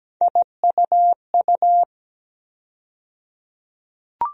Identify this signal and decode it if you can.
Morse code, heard as a keyed tone, about 17 wpm, 700 Hz. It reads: IUU
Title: USA Call Sign Postfixes - 3 Letters 17wpm